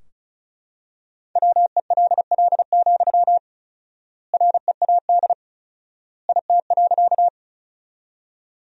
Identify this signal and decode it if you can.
Morse code, heard as a keyed tone, about 34 wpm, 700 Hz. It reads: WELL, READ IT.
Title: 4x Word-Spacing - Sentences from Top 200 Words - 35wpm